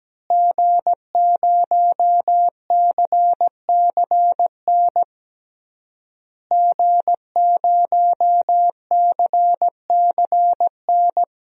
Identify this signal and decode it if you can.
Morse code, heard as a keyed tone, about 17 wpm, 700 Hz. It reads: G0CCN G0CCN
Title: DXCC-100 Call Signs - 5 Characters - Repeated 3x - 17wpm